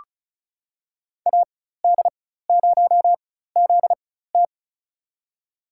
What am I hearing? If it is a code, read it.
Morse code, heard as a keyed tone, about 36 wpm, spaced as for 16 wpm, 700 Hz. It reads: AD0ZT
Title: USA Call Signs - 5 Characters - 4x Character Spacing - 35wpm